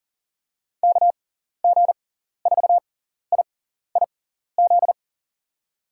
Morse code, heard as a keyed tone, about 40 wpm, spaced as for 14 wpm, 700 Hz. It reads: KG4IIZ
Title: USA Call Signs - 6 Characters - 6x Character Spacing - 40wpm